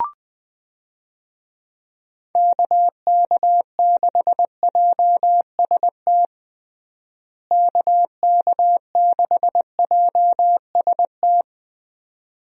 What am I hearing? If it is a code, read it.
Morse code, heard as a keyed tone, about 20 wpm, 700 Hz. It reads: KK6JST KK6JST
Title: USA Call Signs - 6 Characters - Repeated 2x - 20wpm